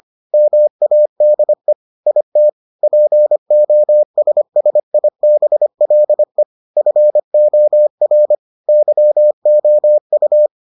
Morse code, heard as a keyed tone, about 25 wpm, 600 Hz. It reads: MADE IT POSSIBLE FOR YOU